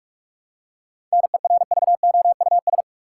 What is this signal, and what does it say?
Morse code, heard as a keyed tone, about 42 wpm, 700 Hz. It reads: NERVOUS
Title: Seven Letter Words - 45wpm